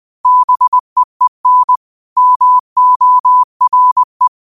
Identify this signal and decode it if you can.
Morse code, heard as a keyed tone, about 20 wpm, 1000 Hz. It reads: BEEN MORE
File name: 20wpm